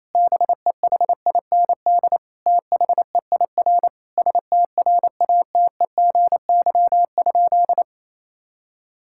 Morse code, heard as a keyed tone, about 28 wpm, 700 Hz. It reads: BEHIND THEIR STRATEGY?
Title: Sentences from Top 1000 Words - Review Part 5 - 28wpm